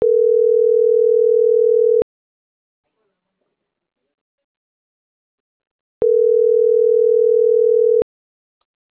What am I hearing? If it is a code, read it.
Ringback tone (North America), 440 + 480 Hz.